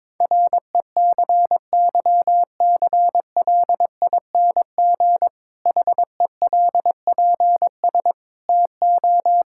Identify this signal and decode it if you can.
Morse code, heard as a keyed tone, about 22 wpm, 700 Hz. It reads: RECYCLING HELPS TO
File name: Difficult Sentences with Hints - 22wpm